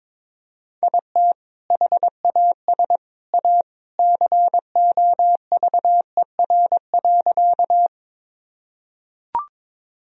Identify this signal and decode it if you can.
Morse code, heard as a keyed tone, about 22 wpm, 700 Hz. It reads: IT HAS A COVER.
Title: Sentences from Top 1000 Words - Review Part 1 - 22wpm